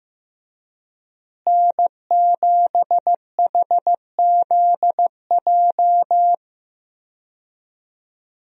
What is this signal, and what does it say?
Morse code, heard as a keyed tone, about 15 wpm, 700 Hz. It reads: N7HZJ
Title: USA Call Signs - 5 Characters - Repeated 3x - 15wpm